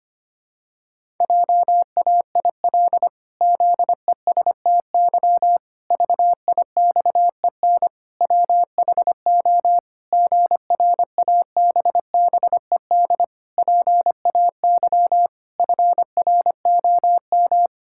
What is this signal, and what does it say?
Morse code, heard as a keyed tone, about 25 wpm, 700 Hz. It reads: JAIL ZESTY VIXEN WHO GRABBED PAY FROM
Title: Pangram Challenge v2 - 25wpm